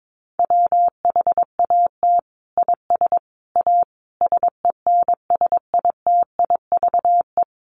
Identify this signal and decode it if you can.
Morse code, heard as a keyed tone, about 22 wpm, 700 Hz. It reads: WHAT IS A SENSITIVE
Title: Sentences from Top 1400 Words 22wpm